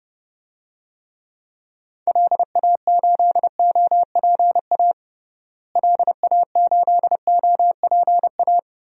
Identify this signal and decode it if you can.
Morse code, heard as a keyed tone, about 30 wpm, 700 Hz. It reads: LA8OPA LA8OPA
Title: DXCC-100 Call Signs - 6 Characters - Repeated 2x - 30wpm